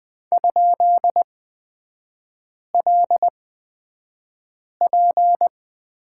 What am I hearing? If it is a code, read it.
Morse code, heard as a keyed tone, about 20 wpm, spaced as for 5 wpm, 700 Hz. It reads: ?LP
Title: 250 Lesson 40e - BK - Three Characters 20wpm